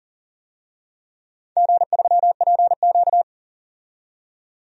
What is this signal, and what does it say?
Morse code, heard as a keyed tone, about 40 wpm, 700 Hz. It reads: G3PQ
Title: DXCC-100 Call Signs - 4 Characters - 40wpm